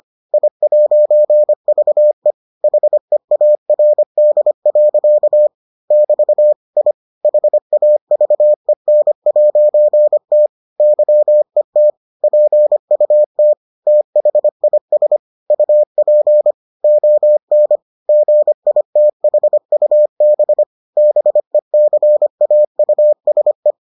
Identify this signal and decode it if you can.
Morse code, heard as a keyed tone, about 25 wpm, 600 Hz. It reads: I'VE HEARD. <BT> I HAVEN'T YET PUT THIS UP ON GITHUB BECAUSE